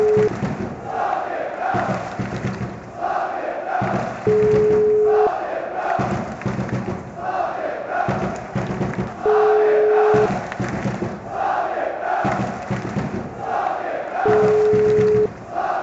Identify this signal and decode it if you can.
Ringback tone (Europe, 425 Hz), with noise.